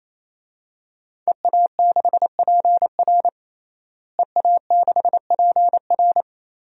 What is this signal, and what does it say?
Morse code, heard as a keyed tone, about 28 wpm, 700 Hz. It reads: EA6PR EA6PR
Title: DXCC-100 Call Signs - 5 Characters - Repeated 3x - 28wpm